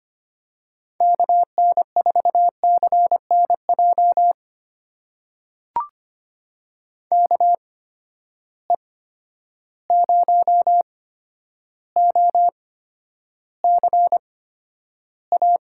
Morse code, heard as a keyed tone, about 25 wpm, spaced as for 11 wpm, 700 Hz. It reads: KN4CNJ KE0OCA
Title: USA Call Signs - 6 Characters - 8x Character Spacing - 25wpm